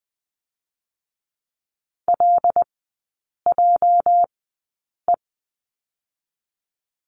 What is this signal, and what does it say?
Morse code, heard as a keyed tone, about 20 wpm, 700 Hz. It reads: L J E
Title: 205 Lesson 33e - Letter j - Three Characters 20wpm